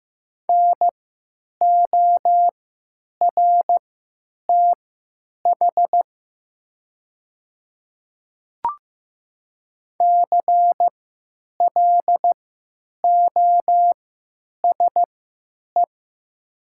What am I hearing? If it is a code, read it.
Morse code, heard as a keyed tone, about 15 wpm, spaced as for 9 wpm, 700 Hz. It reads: NORTH CLOSE
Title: 3x Character Spacing - Top 300 Words - 15wpm